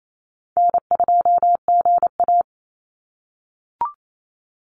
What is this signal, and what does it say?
Morse code, heard as a keyed tone, about 28 wpm, 700 Hz. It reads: N2GA
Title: Call Signs - 28wpm